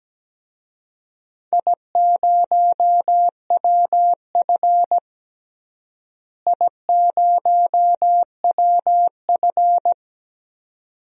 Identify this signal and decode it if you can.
Morse code, heard as a keyed tone, about 17 wpm, 700 Hz. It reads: I0WF I0WF
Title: DXCC-100 Call Signs - 4 Characters - Repeated 2x - 17wpm